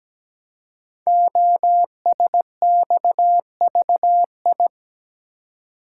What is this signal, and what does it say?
Morse code, heard as a keyed tone, about 17 wpm, 700 Hz. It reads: OSXVI
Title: Code Groups - 5 Letters - 17wpm